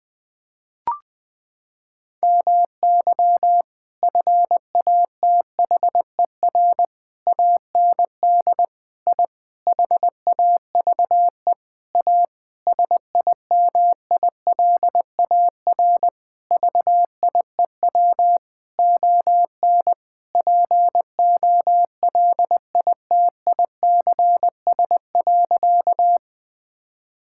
Morse code, heard as a keyed tone, about 21 wpm, 700 Hz. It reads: MY FATHER AND I HAVE A SIMILAR VIEW ON POLITICS.
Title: Sentences from Top 1000 Words - Review Part 6 - 20wpm